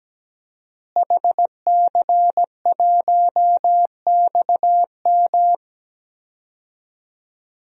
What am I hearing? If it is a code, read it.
Morse code, heard as a keyed tone, about 17 wpm, 700 Hz. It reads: HC1XM
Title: DXCC-100 Call Signs - 5 Characters - Repeated 3x - 17wpm